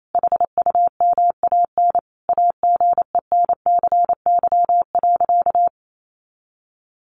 Morse code, heard as a keyed tone, about 28 wpm, 700 Hz. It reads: HUMAN AGENCY.
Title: Sentences from Top 1300 Words 28wpm